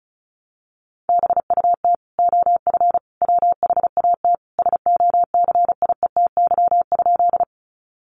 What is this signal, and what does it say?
Morse code, heard as a keyed tone, about 34 wpm, 700 Hz. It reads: BUT OF WHAT SOCIETY?
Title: Sentences from Top 700 Words 35wpm